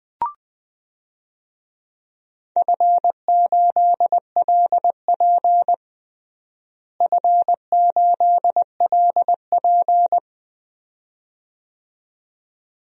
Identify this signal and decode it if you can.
Morse code, heard as a keyed tone, about 20 wpm, 700 Hz. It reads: F8LP F8LP
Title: DXCC-100 Call Signs - 4 Characters - Repeated 2x - 20wpm